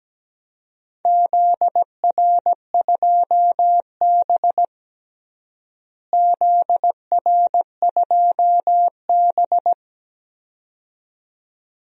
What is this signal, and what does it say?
Morse code, heard as a keyed tone, about 17 wpm, 700 Hz. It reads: ZR2B ZR2B
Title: DXCC-100 Call Signs - 4 Characters - Repeated 2x - 17wpm